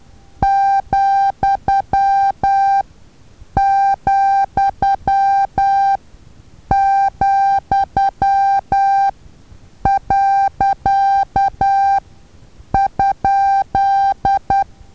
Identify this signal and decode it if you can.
Morse code, heard as a keyed tone, about 10 wpm, spaced as for 7 wpm, 790 Hz. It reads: ,,,.?
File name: Taviro HA6VA A